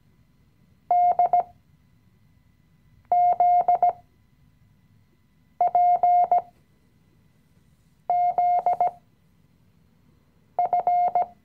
Morse code, heard as a keyed tone, about 17 wpm, spaced as for 5 wpm, 710 Hz. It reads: DZPZF
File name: CW-Online-Kurs / Anfänger / Lektion 018